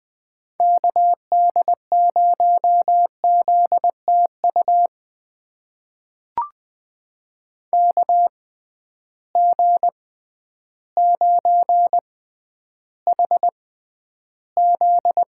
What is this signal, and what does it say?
Morse code, heard as a keyed tone, about 20 wpm, spaced as for 11 wpm, 700 Hz. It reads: KD0ZTU KG9HZ
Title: USA Call Signs - 6x Character Spacing - 20wpm